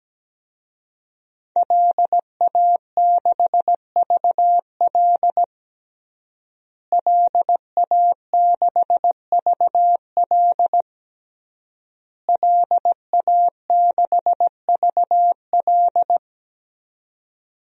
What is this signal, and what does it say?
Morse code, heard as a keyed tone, about 17 wpm, 700 Hz. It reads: LA6VL LA6VL LA6VL
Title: DXCC-100 Call Signs - 5 Characters - Repeated 3x - 17wpm